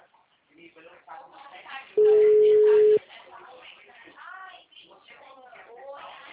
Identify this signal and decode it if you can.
Ringback tone (Europe, 425 Hz), with speech.
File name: out- -2012- - - .18205